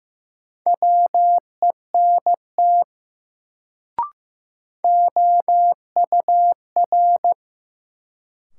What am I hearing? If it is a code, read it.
Morse code, heard as a keyed tone, about 15 wpm, 700 Hz. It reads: WENT OUR
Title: Sentences from Top 200 Words - New Word Drill - 15wpm